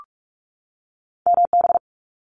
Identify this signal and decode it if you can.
Morse code, heard as a keyed tone, about 46 wpm, 700 Hz. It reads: MB
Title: Canadian Provinces and Territories 45wpm